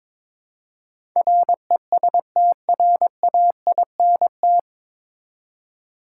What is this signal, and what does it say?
Morse code, heard as a keyed tone, about 22 wpm, 700 Hz. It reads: RESTRAINT
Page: Morse Code Ninja practice 22wpm